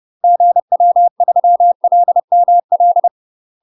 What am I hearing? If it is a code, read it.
Morse code, heard as a keyed tone, about 30 wpm, 700 Hz. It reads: GW3LML
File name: UK MASTER QSOs TRAINING / 30WPM / UK MASTER QSOs 0033